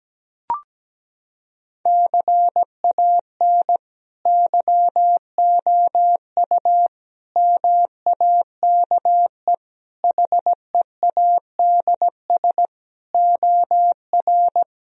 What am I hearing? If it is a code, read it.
Morse code, heard as a keyed tone, about 17 wpm, 700 Hz. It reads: CAN YOU MAKE HEADS OR
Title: Sentences from Top 1000 Words - Review Part 3 - 17wpm